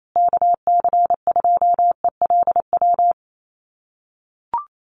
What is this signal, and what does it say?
Morse code, heard as a keyed tone, about 28 wpm, 700 Hz. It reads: KC2ELW
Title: USA Call Signs Repeated 2x 28wpm